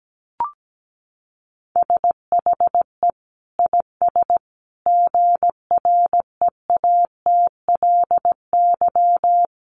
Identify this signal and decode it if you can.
Morse code, heard as a keyed tone, about 17 wpm, 700 Hz. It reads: SHE IS GREATLY
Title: Sentences from Top 1400 Words 17wpm